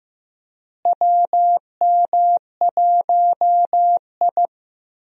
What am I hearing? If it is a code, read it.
Morse code, heard as a keyed tone, about 16 wpm, 700 Hz. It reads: WM1I
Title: USA Call Signs - 4 Characters - Repeated 3x - 15wpm